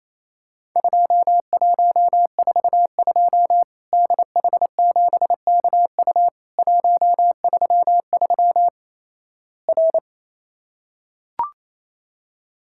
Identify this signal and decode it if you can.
Morse code, heard as a keyed tone, about 28 wpm, 700 Hz. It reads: 2142 DH7KU 133 R
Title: Worked All Europe DX Contest - QTC Exchange - 28wpm